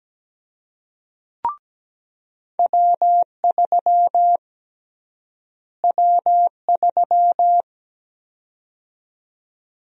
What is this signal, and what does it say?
Morse code, heard as a keyed tone, about 17 wpm, 700 Hz. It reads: W3 W3
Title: USA Call Sign Prefixes - Repeated 2x - 17wpm